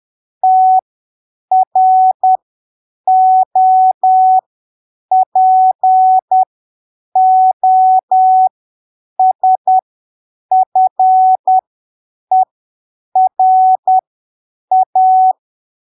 Morse code, heard as a keyed tone, about 10 wpm, spaced as for 7 wpm, 750 Hz. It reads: TROPOSFERA
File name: arquivos / texto / texto 10 10ppm